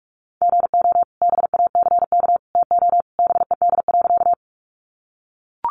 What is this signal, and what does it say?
Morse code, heard as a keyed tone, about 44 wpm, 700 Hz. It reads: GO BACK TO BED.
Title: Sentences from Top 500 Words 45wpm